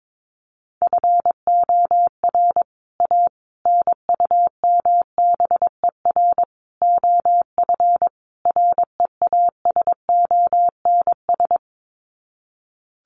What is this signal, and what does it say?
Morse code, heard as a keyed tone, about 22 wpm, 700 Hz. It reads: FOR A NUMBER OF REASONS
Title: Sets of 5 Words Encore2 22wpm